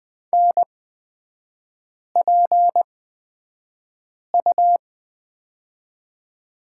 Morse code, heard as a keyed tone, about 20 wpm, spaced as for 5 wpm, 690 Hz. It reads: NPU
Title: 229 Lesson 37e - Letter x - Three Characters 20wpm